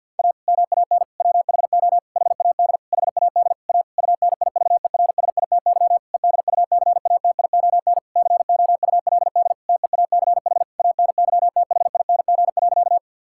Morse code, heard as a keyed tone, about 50 wpm, 700 Hz. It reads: A MAN WHO HAD HAD A UNIVERSITY EDUCATION COULD TEACH ANYTHING.
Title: Sentences from Top 1000 Words - Review Part 6 - 50wpm